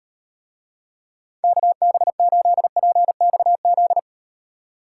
Morse code, heard as a keyed tone, about 36 wpm, 700 Hz. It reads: KB8PXZ